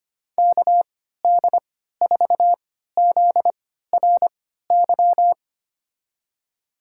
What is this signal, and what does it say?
Morse code, heard as a keyed tone, about 25 wpm, spaced as for 14 wpm, 700 Hz. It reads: KD4ZRY
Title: USA Call Signs - 6 Characters - 3x Character Spacing - 25wpm